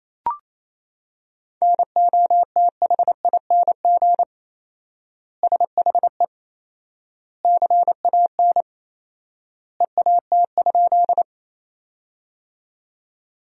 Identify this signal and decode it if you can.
Morse code, heard as a keyed tone, about 28 wpm, 700 Hz. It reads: NOTHING SHE CAN EAT?
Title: 4x Word-Spacing - Sentences from Top 500 Words - 28wpm